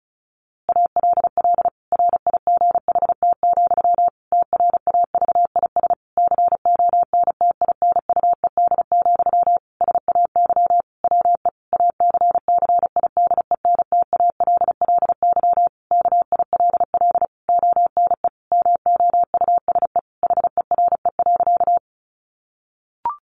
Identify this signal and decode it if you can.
Morse code, heard as a keyed tone, about 34 wpm, 700 Hz. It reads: ALL RIGHT, TRAVIS CONTINUED, SAY WE ACCIDENTALLY KILL ONE MOUSE HERE.
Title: A Sound Of Thunder 35wpm